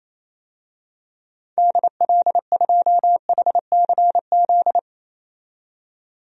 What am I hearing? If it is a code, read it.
Morse code, heard as a keyed tone, about 28 wpm, 700 Hz. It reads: DL2HCZ
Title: DXCC-100 Call Signs - 6 Characters - 28wpm